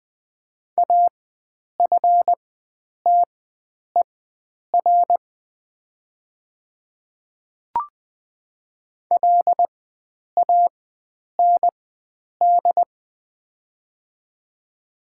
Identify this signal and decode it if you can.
Morse code, heard as a keyed tone, about 20 wpm, spaced as for 9 wpm, 700 Hz. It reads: AFTER LAND
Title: 4x Character Spacing - Top 300 Words - 20wpm